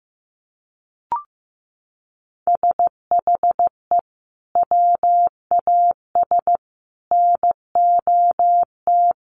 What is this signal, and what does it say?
Morse code, heard as a keyed tone, about 15 wpm, 700 Hz. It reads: SHE WAS NOT
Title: Sets of 5 Words Encore2 15wpm